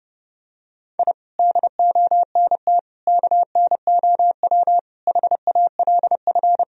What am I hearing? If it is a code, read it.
Morse code, heard as a keyed tone, about 30 wpm, 700 Hz. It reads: I DONT KNOW HALF
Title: Sentences from Top 1000 Words - Review Part 1 - 30wpm